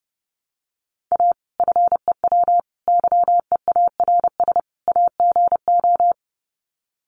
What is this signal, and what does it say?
Morse code, heard as a keyed tone, about 30 wpm, 700 Hz. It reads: A FEW YEARS AGO
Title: Sets of 4 Words 30wpm